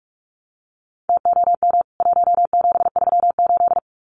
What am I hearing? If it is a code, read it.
Morse code, heard as a keyed tone, about 44 wpm, 700 Hz. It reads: TOM 1738
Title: CWT Exchange 45wpm